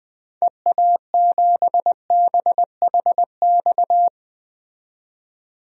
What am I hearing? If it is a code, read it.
Morse code, heard as a keyed tone, about 20 wpm, 700 Hz. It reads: EA7BHX
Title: DXCC-100 Call Signs - 6 Characters - 20wpm